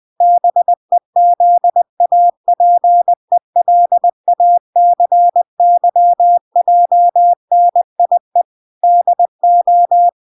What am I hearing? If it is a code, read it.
Morse code, heard as a keyed tone, about 20 wpm, 700 Hz. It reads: BEZAPELACYJNIE DO